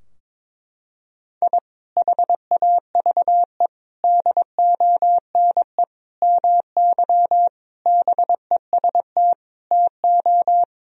Morse code, heard as a keyed tone, about 22 wpm, 700 Hz. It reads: I HAVE DONE MY BEST TO
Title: Sentences from Top 1000 Words - Review Part 4 - 22wpm